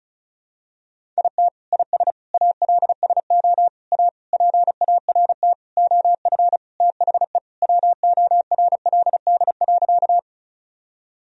Morse code, heard as a keyed tone, about 36 wpm, 700 Hz. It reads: IT IS ALSO A PART OF THE WORLD.
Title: Sentences from Top 1000 Words - Review Part 1 - 35wpm